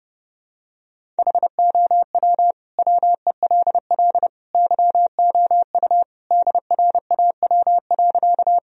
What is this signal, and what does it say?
Morse code, heard as a keyed tone, about 30 wpm, 700 Hz. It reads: HOW WELL YOU DRAW.
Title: Sentences from Top 1000 Words - Review Part 1 - 30wpm